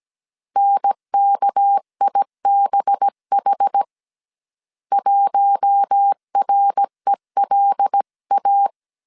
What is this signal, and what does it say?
Morse code, heard as a keyed tone, about 17 wpm, 800 Hz. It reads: NKIBH 1RELA
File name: CW / DET / 1 6 1500C S A 16wpm 2635